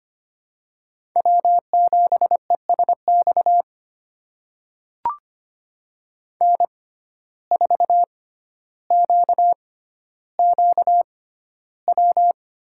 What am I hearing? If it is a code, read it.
Morse code, heard as a keyed tone, about 25 wpm, spaced as for 13 wpm, 700 Hz. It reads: W7ESX N4QQW
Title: USA Call Signs - 5 Characters - 6x Character Spacing - 25wpm